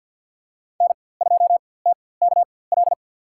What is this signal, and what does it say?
Morse code, heard as a keyed tone, about 48 wpm, spaced as for 23 wpm, 700 Hz. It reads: N2TKL
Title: USA Call Signs - 4x Character Spacing - 50wpm